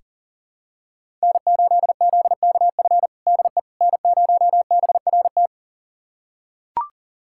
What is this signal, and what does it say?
Morse code, heard as a keyed tone, about 37 wpm, 700 Hz. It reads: N8ZKF DE N0BRT
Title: USA Callsign DE Callsign - 5 Characters - 40wpm